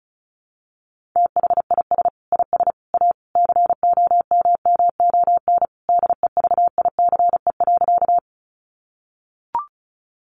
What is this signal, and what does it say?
Morse code, heard as a keyed tone, about 34 wpm, 700 Hz. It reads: THIS IS A COMMON DEVICE.
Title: Sentences from Top 1500 Words 35wpm